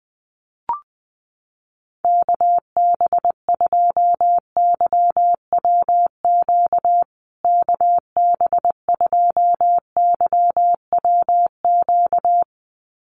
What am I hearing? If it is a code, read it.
Morse code, heard as a keyed tone, about 19 wpm, 700 Hz. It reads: KB2YWQ KB2YWQ
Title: USA Call Signs Repeated 2x 20wpm